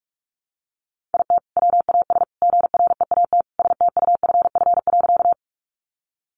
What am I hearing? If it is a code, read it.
Morse code, heard as a keyed tone, about 44 wpm, 700 Hz. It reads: IT WAS GREAT STUFF.
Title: Sentences from Top 1400 Words 45wpm